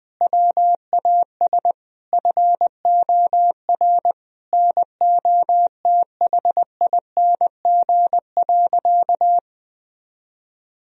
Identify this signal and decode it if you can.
Morse code, heard as a keyed tone, about 21 wpm, 700 Hz. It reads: WAS FOR NOTHING.
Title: Sentences from Top 1000 Words - Review Part 2 - 20wpm